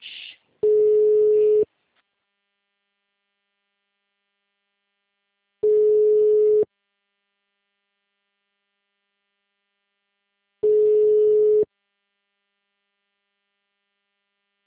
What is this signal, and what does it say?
Ringback tone (Europe, 425 Hz), with speech.
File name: out- -2010- - - .2